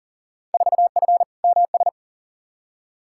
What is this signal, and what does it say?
Morse code, heard as a keyed tone, about 40 wpm, 700 Hz. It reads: 4F MS
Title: Field Day Exchanges Encore - 40wpm